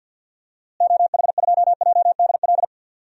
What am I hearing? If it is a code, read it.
Morse code, heard as a keyed tone, about 53 wpm, 700 Hz. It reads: OH2JDL
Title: DXCC-100 Call Signs - 6 Characters - Repeated 2x - 50wpm